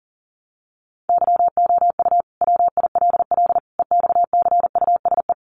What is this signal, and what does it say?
Morse code, heard as a keyed tone, about 40 wpm, 700 Hz. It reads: YOU WILL EXCUSE
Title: Sentences from Top 700 Words 40wpm